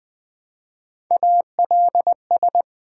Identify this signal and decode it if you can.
Morse code, heard as a keyed tone, about 20 wpm, 700 Hz. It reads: ALS
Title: Build A Word - 1 Letter At A Time - 4-Letter Words - 20wpm